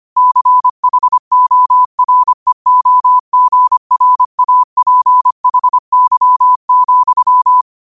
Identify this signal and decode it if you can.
Morse code, heard as a keyed tone, about 25 wpm, 1000 Hz. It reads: CHOREOGRAPHY,